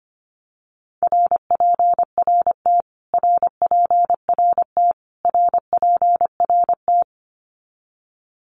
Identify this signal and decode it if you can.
Morse code, heard as a keyed tone, about 25 wpm, 700 Hz. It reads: RPRT RPRT RPRT
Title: Abbreviations Repeated 3x - 25wpm